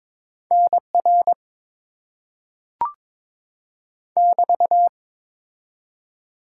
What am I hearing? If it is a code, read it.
Morse code, heard as a keyed tone, about 23 wpm, 700 Hz. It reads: NR <BT>
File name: Top 90 QSO Elements - 6x Character Spacing - 22wpm